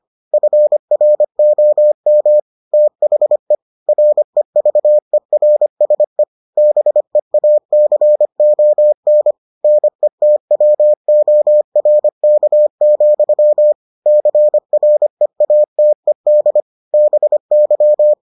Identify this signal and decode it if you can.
Morse code, heard as a keyed tone, about 25 wpm, 600 Hz. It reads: FROM THE REVERSE BEACON NETWORK, CREATED BY